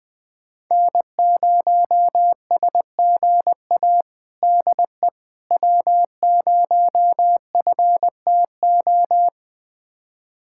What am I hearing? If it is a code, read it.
Morse code, heard as a keyed tone, about 20 wpm, 700 Hz. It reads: N0SGA DE W0FTO
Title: USA Callsign DE Callsign - 5 Characters - 20wpm